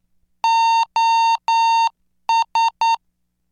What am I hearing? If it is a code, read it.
Morse code, heard as a keyed tone, about 9 wpm, 910 Hz. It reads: OS